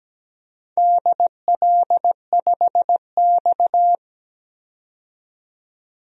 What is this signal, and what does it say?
Morse code, heard as a keyed tone, about 17 wpm, 700 Hz. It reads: DL5X
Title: DXCC-100 Call Signs - 4 Characters - 17wpm